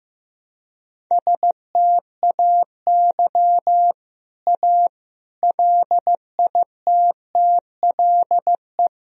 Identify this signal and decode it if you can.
Morse code, heard as a keyed tone, about 15 wpm, 700 Hz. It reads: STAY A LITTLE
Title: Sentences from Top 1000 Words - Review Part 1 - 15wpm